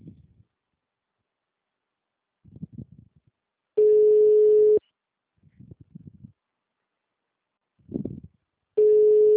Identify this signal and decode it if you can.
Ringback tone (Europe, 425 Hz).